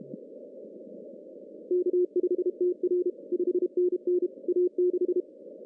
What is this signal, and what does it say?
Morse code, heard as a keyed tone, about 28 wpm, 360 Hz. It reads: K5TR 5NN A6
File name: CW / cqww-cw-k5tr